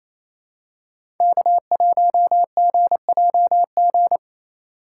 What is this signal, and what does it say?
Morse code, heard as a keyed tone, about 27 wpm, 700 Hz. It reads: K1GJG
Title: USA Call Signs - 5 Characters - 4x Character Spacing - 28wpm